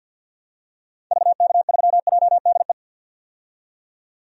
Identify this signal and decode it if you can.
Morse code, heard as a keyed tone, about 52 wpm, 700 Hz. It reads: VK3JDE
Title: DXCC-100 Call Signs - 6 Characters - 50wpm